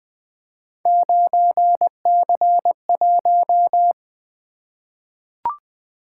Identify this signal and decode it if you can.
Morse code, heard as a keyed tone, about 20 wpm, 700 Hz. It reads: 9C1
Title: Three Letter-Number Combinations - 20wpm